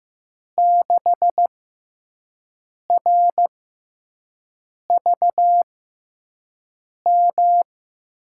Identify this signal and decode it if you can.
Morse code, heard as a keyed tone, about 15 wpm, spaced as for 5 wpm, 700 Hz. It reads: 6RVM
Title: USA Call Signs - 5 Characters - 6x Character Spacing - 15wpm